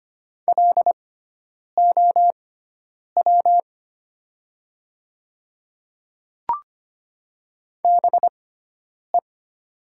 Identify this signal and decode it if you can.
Morse code, heard as a keyed tone, about 25 wpm, spaced as for 9 wpm, 700 Hz. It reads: LOW BE
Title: 6x Character Spacing - Top 300 Words - 25wpm